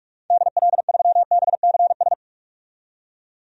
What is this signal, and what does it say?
Morse code, heard as a keyed tone, about 44 wpm, 690 Hz. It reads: DL3BCS
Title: DXCC-100 Call Signs - 6 Characters - 45wpm